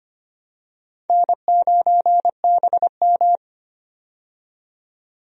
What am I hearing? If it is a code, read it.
Morse code, heard as a keyed tone, about 25 wpm, 700 Hz. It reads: N9BM
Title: USA Call Signs - 4 Characters - 25wpm